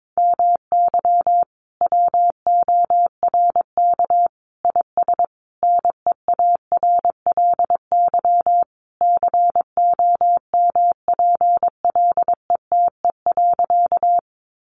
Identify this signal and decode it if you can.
Morse code, heard as a keyed tone, about 22 wpm, 700 Hz. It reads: MY WORK IS NEARLY COMPLETE.
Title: Sentences from Top 1300 Words 22wpm